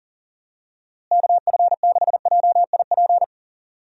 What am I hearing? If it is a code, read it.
Morse code, heard as a keyed tone, about 40 wpm, 700 Hz. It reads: KF6JIP
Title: USA Call Signs - 6 Characters - Repeated 3x - 40wpm